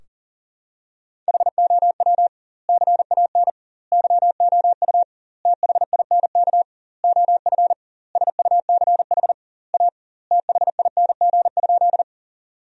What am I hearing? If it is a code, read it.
Morse code, heard as a keyed tone, about 40 wpm, 700 Hz. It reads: HOW CAN YOU THINK OF SUCH A THING?
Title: 2x Word-Spacing - Sentences from Top 200 Words - 40wpm